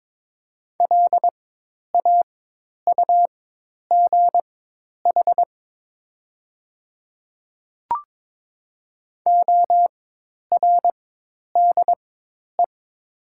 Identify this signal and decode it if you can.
Morse code, heard as a keyed tone, about 22 wpm, spaced as for 10 wpm, 700 Hz. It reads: LAUGH ORDE
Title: 4x Character Spacing - Top 500 Words - 22wpm